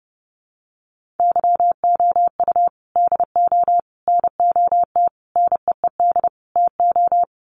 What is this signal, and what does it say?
Morse code, heard as a keyed tone, about 30 wpm, 700 Hz. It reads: YOU DO NOT NEED TO
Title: Sets of 5 Words 30wpm